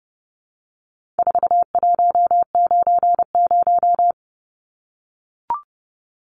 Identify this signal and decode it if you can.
Morse code, heard as a keyed tone, about 30 wpm, 700 Hz. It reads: 4190
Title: Numbers - 4-Digit 30wpm